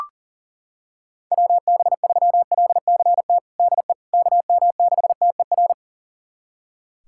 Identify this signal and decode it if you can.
Morse code, heard as a keyed tone, about 40 wpm, 700 Hz. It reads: WB3LCT DE KM6TER
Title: USA Callsign DE Callsign 40wpm